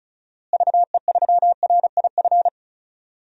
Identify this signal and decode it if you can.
Morse code, heard as a keyed tone, about 36 wpm, 700 Hz. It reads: VE3RIF
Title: DXCC-100 Call Signs - 6 Characters - Repeated 3x - 35wpm